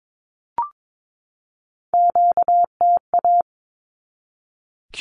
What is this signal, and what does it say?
Morse code, heard as a keyed tone, about 22 wpm, 700 Hz. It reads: QTA
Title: Q-Codes 22wpm